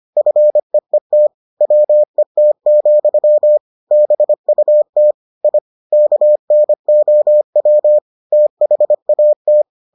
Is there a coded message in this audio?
Yes: Morse code from 0.2 to 9.6 s, heard as a keyed tone, about 25 wpm, 600 Hz. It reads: FEET WET, BUT I KNOW THAT